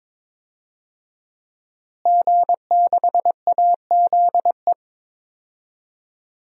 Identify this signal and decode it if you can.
Morse code, heard as a keyed tone, about 21 wpm, 700 Hz. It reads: G6AZE
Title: DXCC-100 Call Signs - 5 Characters - Repeated 2x - 22wpm